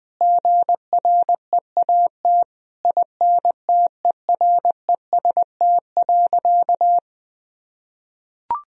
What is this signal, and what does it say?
Morse code, heard as a keyed tone, about 20 wpm, 700 Hz. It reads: GREAT INTEREST.
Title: Sentences from Top 1000 Words - Review Part 2 - 20wpm